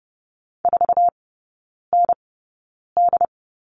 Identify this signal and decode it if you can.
Morse code, heard as a keyed tone, about 30 wpm, spaced as for 9 wpm, 700 Hz. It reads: 4ND
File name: Lesson 19e - Letter w - Three Characters 30wpm